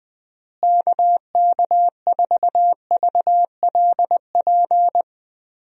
Morse code, heard as a keyed tone, about 20 wpm, 700 Hz. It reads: KK4VLP
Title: USA Call Signs - 6 Characters - Repeated 3x - 20wpm